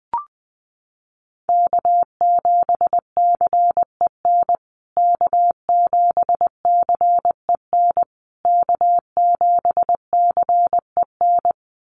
Morse code, heard as a keyed tone, about 20 wpm, 700 Hz. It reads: K7CEN K7CEN K7CEN
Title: USA Call Signs Repeated 3x 20wpm